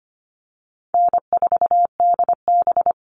Morse code, heard as a keyed tone, about 25 wpm, 700 Hz. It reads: N4DB
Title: All Cut Numbers - 4-Digit 25wpm